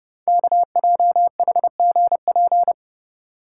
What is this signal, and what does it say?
Morse code, heard as a keyed tone, about 30 wpm, 700 Hz. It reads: KJHGP
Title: Code Groups - 5 Letters - 30wpm